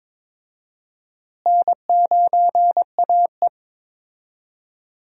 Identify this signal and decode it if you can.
Morse code, heard as a keyed tone, about 22 wpm, 700 Hz. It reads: N9AE
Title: USA Call Signs - 4 Characters - Repeated 2x - 22wpm